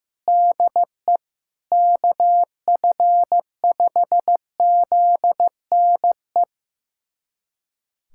Morse code, heard as a keyed tone, about 16 wpm, 700 Hz. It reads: DE KF5ZNE
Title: USA Callsign DE Callsign 15wpm